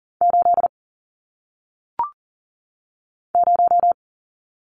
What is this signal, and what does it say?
Morse code, heard as a keyed tone, about 39 wpm, 700 Hz. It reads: 8 0
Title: Numbers - 1-Digit 40wpm